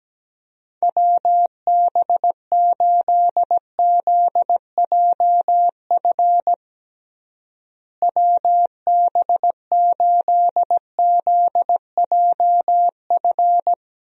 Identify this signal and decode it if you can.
Morse code, heard as a keyed tone, about 17 wpm, 700 Hz. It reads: WB8ZJF WB8ZJF
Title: USA Call Signs - 6 Characters - Repeated 3x - 17wpm